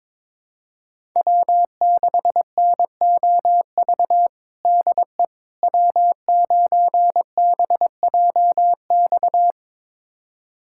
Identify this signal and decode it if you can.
Morse code, heard as a keyed tone, about 21 wpm, 700 Hz. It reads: W6NOV DE W9BJX
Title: USA Callsign DE Callsign - 5 Characters - 22wpm